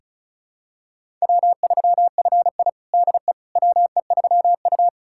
Morse code, heard as a keyed tone, about 34 wpm, 700 Hz. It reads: W3FI DE WE3U
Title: USA Callsign DE Callsign - 4 Characters - 35wpm